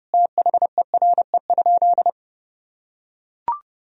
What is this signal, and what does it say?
Morse code, heard as a keyed tone, about 31 wpm, 700 Hz. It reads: THERE?
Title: Sentences from Top 100 Words and Abbreviations 30wpm